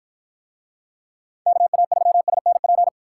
Morse code, heard as a keyed tone, about 53 wpm, 700 Hz. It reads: KA3SNP